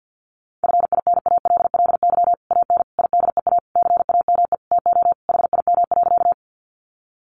Morse code, heard as a keyed tone, about 50 wpm, 700 Hz. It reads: FINALLY AN IDEA CAME TO HIM.
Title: Sentences from Top 1000 Words 50wpm